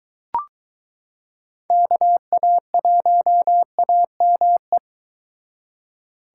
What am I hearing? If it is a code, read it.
Morse code, heard as a keyed tone, about 23 wpm, 700 Hz. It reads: KA1AME